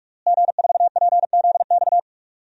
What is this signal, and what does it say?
Morse code, heard as a keyed tone, about 46 wpm, 700 Hz. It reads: G4PZX
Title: DXCC-100 Call Signs - 5 Characters - Repeated 3x - 45wpm